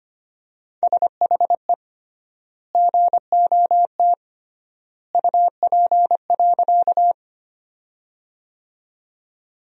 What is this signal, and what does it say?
Morse code, heard as a keyed tone, about 25 wpm, 700 Hz. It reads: SHE GOT UP.
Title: 3x Word-Spacing - Sentences from Top 300 Words - 25wpm